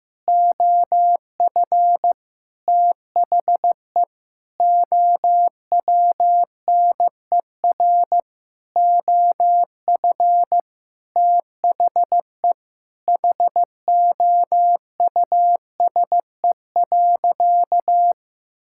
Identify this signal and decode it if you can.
Morse code, heard as a keyed tone, about 16 wpm, 700 Hz. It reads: OF THE OWNER OF THE HOUSE.
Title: Sentences from Top 1600 Words 15wpm